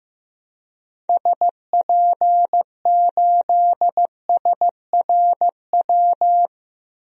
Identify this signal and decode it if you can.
Morse code, heard as a keyed tone, about 15 wpm, 700 Hz. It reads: SP8SRW
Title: DXCC-100 Call Signs - 6 Characters - Repeated 3x - 15wpm